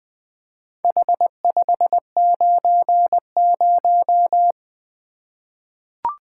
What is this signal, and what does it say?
Morse code, heard as a keyed tone, about 21 wpm, 700 Hz. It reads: H590
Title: Four Letter-Number Combinations - 20wpm